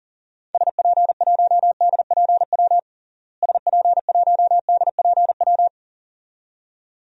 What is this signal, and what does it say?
Morse code, heard as a keyed tone, about 40 wpm, 700 Hz. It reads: SP1DPW SP1DPW
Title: DXCC-100 Call Signs - 6 Characters - Repeated 2x - 40wpm